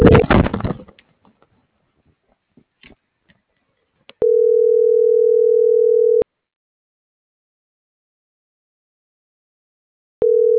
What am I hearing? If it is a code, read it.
Ringback tone (North America), 440 + 480 Hz.